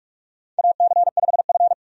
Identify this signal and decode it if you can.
Morse code, heard as a keyed tone, about 47 wpm, 700 Hz. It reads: AC5F
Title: USA Call Signs - 4 Characters - 45wpm